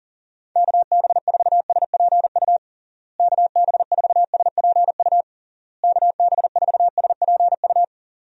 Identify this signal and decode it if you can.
Morse code, heard as a keyed tone, about 40 wpm, 700 Hz. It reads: KB4SPU KB4SPU KB4SPU
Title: USA Call Signs - 6 Characters - Repeated 3x - 40wpm